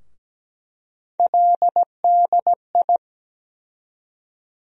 Morse code, heard as a keyed tone, about 17 wpm, 700 Hz. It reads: LDI
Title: USA Call Sign Postfixes - 3 Letters 17wpm